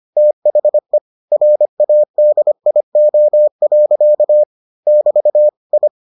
Morse code, heard as a keyed tone, about 25 wpm, 600 Hz. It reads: THE RADIO. <BT> I